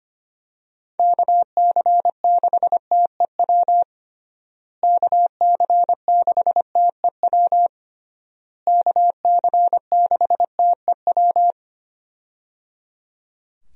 Morse code, heard as a keyed tone, about 25 wpm, 700 Hz. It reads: KC6TEW KC6TEW KC6TEW
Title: USA Call Signs - 6 Characters - Repeated 3x - 25wpm